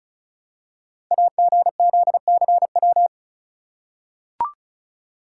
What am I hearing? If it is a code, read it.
Morse code, heard as a keyed tone, about 36 wpm, 690 Hz. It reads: AGZCW
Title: Code Groups - 5 Letters - 35wpm